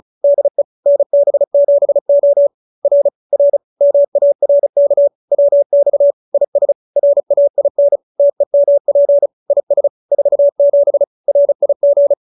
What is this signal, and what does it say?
Morse code, heard as a keyed tone, about 35 wpm, 570 Hz. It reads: DE NB7O R R MARK WX IS RAIN TEMP IS 47 RIG